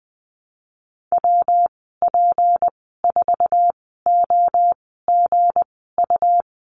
Morse code, heard as a keyed tone, about 20 wpm, spaced as for 14 wpm, 700 Hz. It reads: WP4OGU
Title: 186 Lesson 30g - Letter b - Call Signs 20wpm